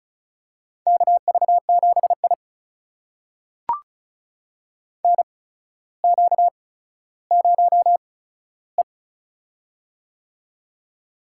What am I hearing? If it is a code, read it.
Morse code, heard as a keyed tone, about 34 wpm, spaced as for 15 wpm, 700 Hz. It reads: KV7I NQ0E
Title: USA Call Signs - 4 Characters - 8x Character Spacing - 35wpm